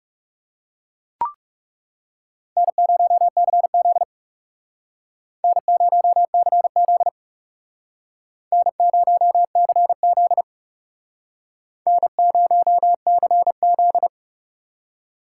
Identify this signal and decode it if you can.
Morse code, heard as a keyed tone, about 39 wpm, 700 Hz. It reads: N0CZ N0CZ N0CZ N0CZ